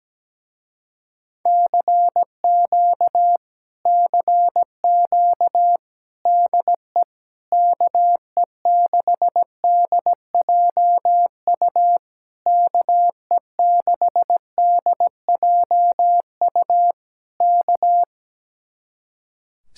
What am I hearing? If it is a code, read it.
Morse code, heard as a keyed tone, about 17 wpm, 700 Hz. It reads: CQ CQ DE KE6DJU KE6DJU K
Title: Speed-Racing Common QSO Phrases Encore - 17wpm